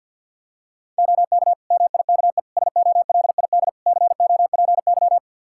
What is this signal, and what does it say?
Morse code, heard as a keyed tone, about 50 wpm, 700 Hz. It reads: OK MIKE SOLID COPY
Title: Practice QSO - Exchange 3 - More Info - 50wpm